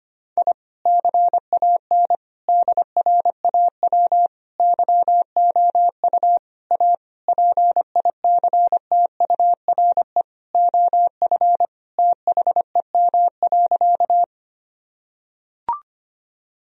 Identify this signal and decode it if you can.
Morse code, heard as a keyed tone, about 25 wpm, 700 Hz. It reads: I CAN DRAW YOU A PICTURE OF THEM.
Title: Sentences from Top 1000 Words - Review Part 1 - 25wpm